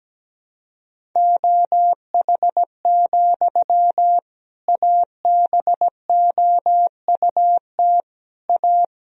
Morse code, heard as a keyed tone, about 17 wpm, 700 Hz. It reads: OH, ABOUT A
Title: Sentences from Top 1000 Words - Review Part 2 - 17wpm